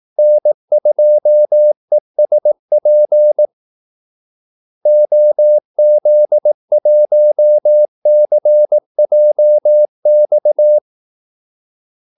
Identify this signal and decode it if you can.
Morse code, heard as a keyed tone, about 18 wpm, 600 Hz. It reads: N2ESP OZ1CJX
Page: Kaldesignaler 18 wpm